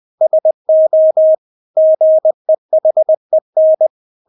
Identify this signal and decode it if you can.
Morse code, heard as a keyed tone, about 20 wpm, 630 Hz. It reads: SO GEHEN
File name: Witz1 20wpm